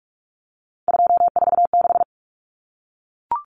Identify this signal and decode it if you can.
Morse code, heard as a keyed tone, about 44 wpm, 700 Hz. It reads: 246
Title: Common Cut Numbers - 3-Digit 45wpm